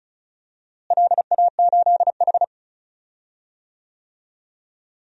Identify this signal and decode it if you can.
Morse code, heard as a keyed tone, about 36 wpm, 700 Hz. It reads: LA8H
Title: DXCC-100 Call Signs - 4 Characters - Repeated 2x - 35wpm